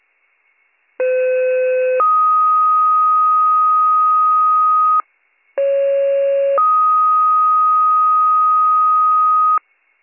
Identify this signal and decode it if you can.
Two-tone page, heard as a one-second tone, then a three-second tone, about 510 then 1250 Hz, about 550 then 1250 Hz. It is heard from 1.0 to 9.6 s.